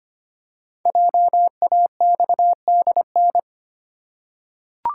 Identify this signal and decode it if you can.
Morse code, heard as a keyed tone, about 25 wpm, 700 Hz. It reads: JAXDN
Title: Code Groups - 5 Letters - 25wpm